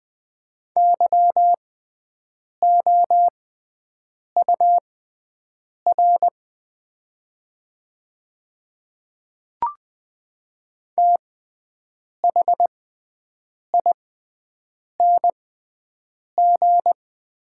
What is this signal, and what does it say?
Morse code, heard as a keyed tone, about 20 wpm, spaced as for 7 wpm, 700 Hz. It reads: YOUR THING
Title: 6x Character Spacing - Top 100 Words - 20wpm